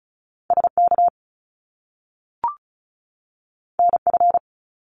Morse code, heard as a keyed tone, about 36 wpm, 700 Hz. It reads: SK NF
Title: Top Two Letter Combinations in Words 35wpm